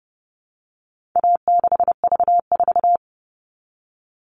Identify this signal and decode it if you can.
Morse code, heard as a keyed tone, about 30 wpm, 700 Hz. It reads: A6V4
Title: All Cut Numbers - 4-Digit 30wpm